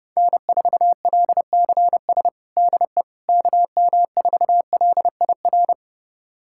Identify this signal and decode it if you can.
Morse code, heard as a keyed tone, about 30 wpm, 700 Hz. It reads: N4LCS DE KM4LIR
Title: USA Callsign DE Callsign 30wpm